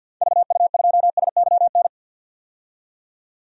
Morse code, heard as a keyed tone, about 46 wpm, 700 Hz. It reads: VU2SYN
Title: DXCC-100 Call Signs - 6 Characters - 50wpm